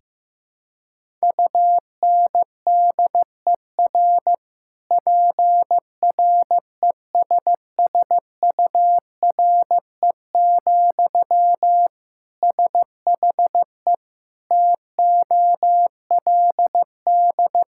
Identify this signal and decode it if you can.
Morse code, heard as a keyed tone, about 15 wpm, 700 Hz. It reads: UNDER PRESSURE, SHE TOLD
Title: Sentences from Top 1000 Words - Review Part 6 - 15wpm